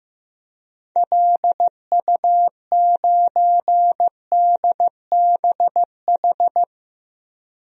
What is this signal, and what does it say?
Morse code, heard as a keyed tone, about 15 wpm, 700 Hz. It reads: LU9DBH
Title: DXCC-100 Call Signs - 6 Characters - Repeated 2x - 15wpm